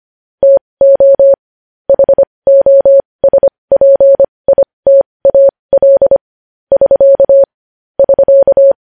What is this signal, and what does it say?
Morse code, heard as a keyed tone, about 25 wpm, spaced as for 20 wpm, 550 Hz. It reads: TO HOSPITAL <SK> <SK>
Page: News Headlines in Morse Code at 20 WPM